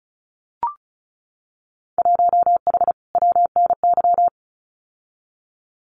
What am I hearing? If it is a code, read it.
Morse code, heard as a keyed tone, about 36 wpm, 700 Hz. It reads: 1H WNY
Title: Winter Field Day Exchanges 35wpm